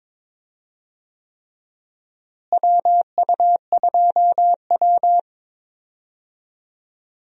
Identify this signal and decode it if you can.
Morse code, heard as a keyed tone, about 21 wpm, 700 Hz. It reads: WU2W
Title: USA Call Signs - 4 Characters - Repeated 3x - 22wpm